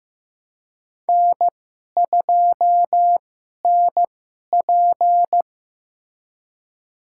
Morse code, heard as a keyed tone, about 15 wpm, spaced as for 11 wpm, 700 Hz. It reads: N2NP
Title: USA Call Signs - 4 Characters - 2x Character Spacing - 15wpm